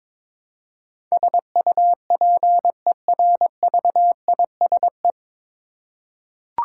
Morse code, heard as a keyed tone, about 22 wpm, 700 Hz. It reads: SUPERVISE
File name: 9 Letter Words - First 3 Letters Given - 22wpm